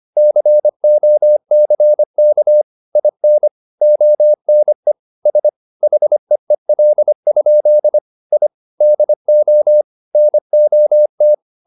Morse code, heard as a keyed tone, about 25 wpm, 600 Hz. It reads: COCK IN ONE S HEEL? I DO NOT
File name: the old man and the sea part 1 25wpm